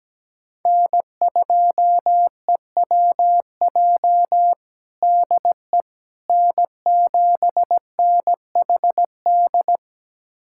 Morse code, heard as a keyed tone, about 17 wpm, 700 Hz. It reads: N2EWJ DE N7NHD
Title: USA Callsign DE Callsign - 5 Characters - 17wpm